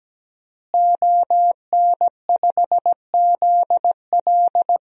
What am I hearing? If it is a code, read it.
Morse code, heard as a keyed tone, about 17 wpm, 700 Hz. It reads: ON5ZL
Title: DXCC-100 Call Signs - 5 Characters - Repeated 2x - 17wpm